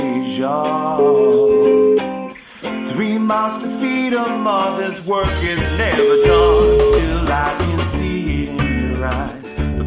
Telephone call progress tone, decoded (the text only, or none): ringback tone